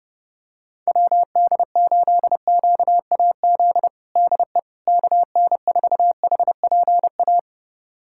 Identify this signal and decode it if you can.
Morse code, heard as a keyed tone, about 30 wpm, 700 Hz. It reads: WD8QAZ DE KN4HPA
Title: USA Callsign DE Callsign - 6 Characters - 30wpm